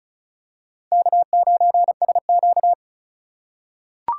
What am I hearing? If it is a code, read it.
Morse code, heard as a keyed tone, about 35 wpm, 690 Hz. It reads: K9SQ
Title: USA Call Signs - 4 Characters - 2x Character Spacing - 35wpm